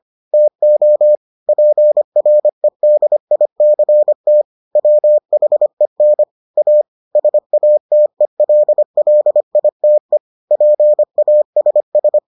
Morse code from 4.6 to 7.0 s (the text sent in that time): WHEN A